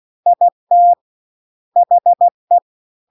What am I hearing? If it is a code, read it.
Morse code, heard as a keyed tone, about 16 wpm, 700 Hz. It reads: IT HE
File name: CW-The-Big-Twoheaded-River-2-16WPM